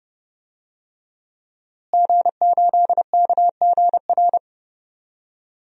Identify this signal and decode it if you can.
Morse code, heard as a keyed tone, about 31 wpm, 700 Hz. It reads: G8KGR
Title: DXCC-100 Call Signs - 5 Characters - Repeated 2x - 30wpm